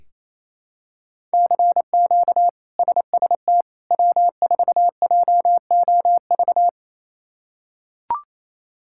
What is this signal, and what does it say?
Morse code, heard as a keyed tone, about 28 wpm, 700 Hz. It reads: CQ SST W4JOV
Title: K1USN SST QSO Practice - 28wpm